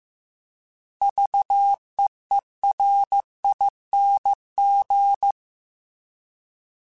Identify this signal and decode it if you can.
Morse code, heard as a keyed tone, about 15 wpm, 800 Hz. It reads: VEERING
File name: Alphabetical / Alphabetical mp3s / Vs @13